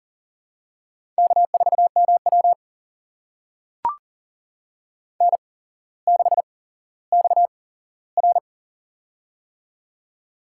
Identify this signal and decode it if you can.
Morse code, heard as a keyed tone, about 40 wpm, spaced as for 17 wpm, 700 Hz. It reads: K4MW N6XR
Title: USA Call Signs - 8x Character Spacing - 40wpm